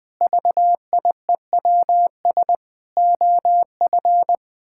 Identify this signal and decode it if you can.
Morse code, heard as a keyed tone, about 20 wpm, 700 Hz. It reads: VIEWS OF
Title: Sentences with 8-Word Phrases - 20wpm